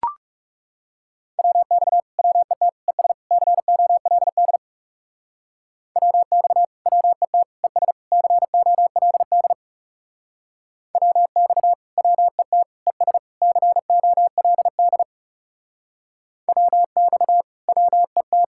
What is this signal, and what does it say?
Morse code, heard as a keyed tone, about 37 wpm, 700 Hz. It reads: WX WET ES COLD WX WET ES COLD WX WET ES COLD WX WET
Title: Speed-Racing Common QSO Phrases - 30wpm